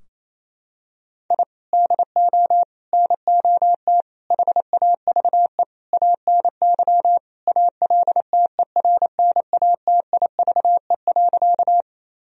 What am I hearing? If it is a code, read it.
Morse code, heard as a keyed tone, about 28 wpm, 700 Hz. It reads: I DO NOT HAVE ANY ALTERNATIVE.
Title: Sentences from Top 1000 Words - Review Part 4 - 28wpm